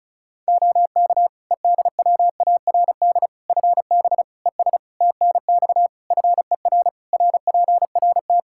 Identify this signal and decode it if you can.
Morse code, heard as a keyed tone, about 36 wpm, 700 Hz. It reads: OK EDWARD FB ES TNX FER RPRT
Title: Practice QSO - Exchange 2 - Info - 35wpm